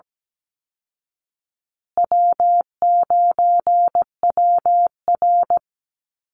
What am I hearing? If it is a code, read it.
Morse code, heard as a keyed tone, about 17 wpm, 700 Hz. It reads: W9WR
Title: Easy Call Signs Repeated 3x - 17wpm